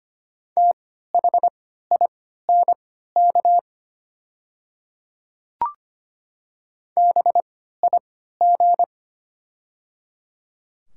Morse code, heard as a keyed tone, about 25 wpm, spaced as for 14 wpm, 700 Hz. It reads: THINK BIG
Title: 3x Character Spacing - Top 500 Words - 25wpm